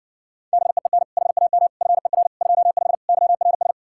Morse code, heard as a keyed tone, about 69 wpm, 700 Hz. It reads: BEEN HAM FER 25 YRS